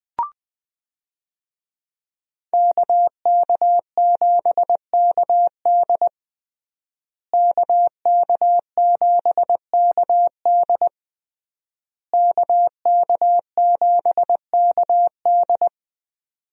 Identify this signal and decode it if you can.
Morse code, heard as a keyed tone, about 20 wpm, 700 Hz. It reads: KK7KD KK7KD KK7KD
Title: USA Call Signs - 5 Characters - Repeated 3x - 20wpm